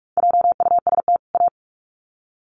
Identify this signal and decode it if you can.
Morse code, heard as a keyed tone, about 44 wpm, 700 Hz. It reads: JUST A
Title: Sets of 2 Words 45wpm